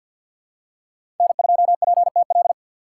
Morse code, heard as a keyed tone, about 53 wpm, 700 Hz. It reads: N2PTL
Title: USA Call Signs - 2x Character Spacing - 50wpm